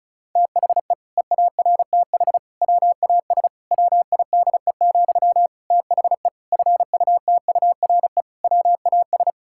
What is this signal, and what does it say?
Morse code, heard as a keyed tone, about 34 wpm, 700 Hz. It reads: THE EARTH WAS WIDE, THE FUTURE WAS
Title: Sentences from Top 1000 Words - Review Part 6 - 35wpm